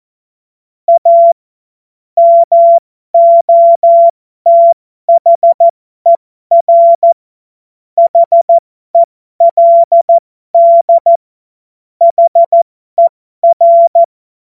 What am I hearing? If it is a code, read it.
Morse code, heard as a keyed tone, about 14 wpm, spaced as for 12 wpm, 680 Hz. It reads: A MOTHER HELD HER
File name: Morsekurs / Witze